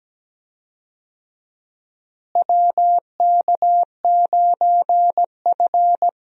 Morse code, heard as a keyed tone, about 17 wpm, 700 Hz. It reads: WK9F
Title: USA Call Signs - 4 Characters - Repeated 3x - 17wpm